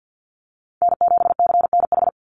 Morse code, heard as a keyed tone, about 50 wpm, 700 Hz. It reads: N7CNH
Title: NA SOTA Chaser Callsigns 50wpm